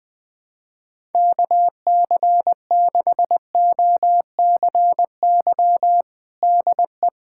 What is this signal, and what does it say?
Morse code, heard as a keyed tone, about 21 wpm, 700 Hz. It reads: KC6OCY DE
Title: USA Callsign DE Callsign - 6 Characters - 20wpm